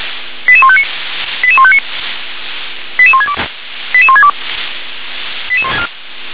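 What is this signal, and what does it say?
Five-tone selective call (ZVEI), heard as quick runs of five tones, tones of about 70 ms. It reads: 88159 88159 88162 88162
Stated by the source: Five-tone ZVEI selective call received on SDR.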